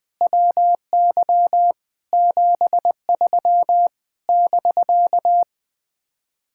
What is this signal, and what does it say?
Morse code, heard as a keyed tone, about 20 wpm, 700 Hz. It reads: WY 73 <BK>
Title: Speed-Racing Common QSO Phrases Encore - 20wpm